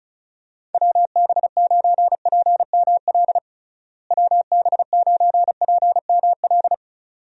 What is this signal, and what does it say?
Morse code, heard as a keyed tone, about 34 wpm, 700 Hz. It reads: WB9PML WB9PML
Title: USA Call Signs - 6 Characters - Repeated 2x - 35wpm